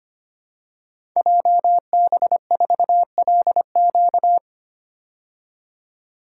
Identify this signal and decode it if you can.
Morse code, heard as a keyed tone, about 25 wpm, 700 Hz. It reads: JB4LQ
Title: Code Groups - 5 Letter:Numbers - 25wpm